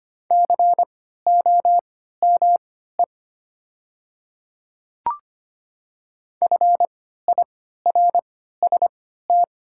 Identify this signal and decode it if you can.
Morse code, heard as a keyed tone, about 25 wpm, spaced as for 14 wpm, 690 Hz. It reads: COME FIRST
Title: 3x Character Spacing - Top 200 Words - 25wpm